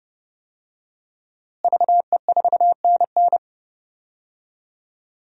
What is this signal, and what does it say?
Morse code, heard as a keyed tone, about 32 wpm, 700 Hz. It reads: VE4NN
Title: DXCC-100 Call Signs - 5 Characters - Repeated 2x - 30wpm